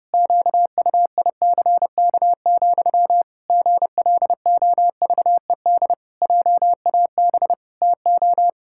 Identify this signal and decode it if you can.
Morse code, heard as a keyed tone, about 30 wpm, 700 Hz. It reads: QUICK, GLOVED JAB TO
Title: Pangram Challenge v2 - 30wpm